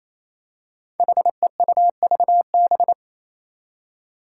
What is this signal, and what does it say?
Morse code, heard as a keyed tone, about 28 wpm, 700 Hz. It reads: HEUVB
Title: Five Challenging Letter-Number Combinations 28wpm